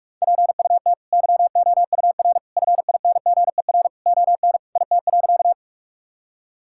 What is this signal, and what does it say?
Morse code, heard as a keyed tone, about 42 wpm, 700 Hz. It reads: PUT YOUR FINGER ON IT.
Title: Sentences from Top 1600 Words 45wpm